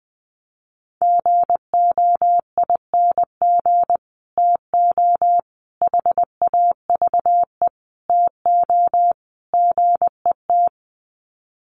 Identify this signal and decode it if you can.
Morse code, heard as a keyed tone, about 20 wpm, 700 Hz. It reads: GOING TO HAVE TO GET
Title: Sets of 5 Words Encore2 20wpm